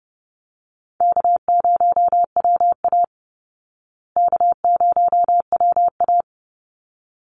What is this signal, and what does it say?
Morse code, heard as a keyed tone, about 30 wpm, 700 Hz. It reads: K0WA K0WA
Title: Easy Call Signs Repeated 2x - 30wpm